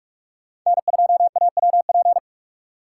Morse code, heard as a keyed tone, about 45 wpm, 700 Hz. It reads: N2AWP
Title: USA Call Signs - 8x Character Spacing - 45wpm